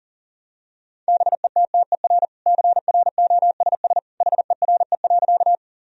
Morse code, heard as a keyed tone, about 40 wpm, 700 Hz. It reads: BETTER CROSS HERE.
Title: Sentences from Top 1000 Words - Review Part 1 - 40wpm